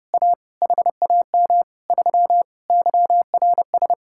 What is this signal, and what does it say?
Morse code, heard as a keyed tone, about 30 wpm, 700 Hz. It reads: A HAM 3 YRS